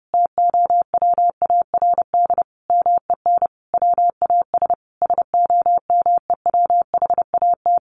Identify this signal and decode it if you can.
Morse code, heard as a keyed tone, about 30 wpm, 700 Hz. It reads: TOWARD MEN WAS SOMEWHAT
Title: Sentences from Top 1300 Words 30wpm